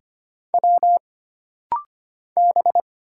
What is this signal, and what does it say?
Morse code, heard as a keyed tone, about 25 wpm, 700 Hz. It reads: W B
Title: Single Letters - Rapid-Fire 25wpm